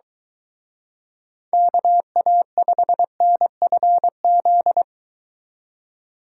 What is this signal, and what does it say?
Morse code, heard as a keyed tone, about 23 wpm, 700 Hz. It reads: KA5NFZ